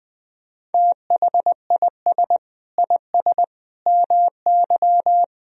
Morse code, heard as a keyed tone, about 20 wpm, 700 Hz. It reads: THIS IS MY
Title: Sentences from Top 1000 Words - Review Part 1 - 20wpm